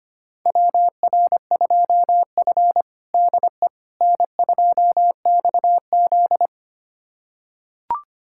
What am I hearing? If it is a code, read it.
Morse code, heard as a keyed tone, about 25 wpm, 700 Hz. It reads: WR2F DE N2XZ
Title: USA Callsign DE Callsign - 4 Characters - 25wpm